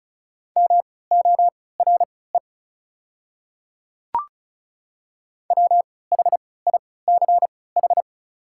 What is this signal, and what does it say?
Morse code, heard as a keyed tone, about 36 wpm, spaced as for 20 wpm, 700 Hz. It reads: MORE WHICH
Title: 3x Character Spacing - Top 200 Words - 35wpm